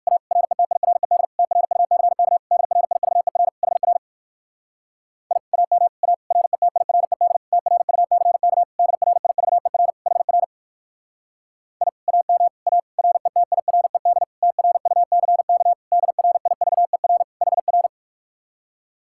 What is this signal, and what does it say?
Morse code, heard as a keyed tone, about 54 wpm, 700 Hz. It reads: A RETIRED TRUCK DRIVER HR I AM A RETIRED TRUCK DRIVER HR I AM A RETIRED TRUCK DRIVER HR
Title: Speed-Racing Common QSO Phrases - 45wpm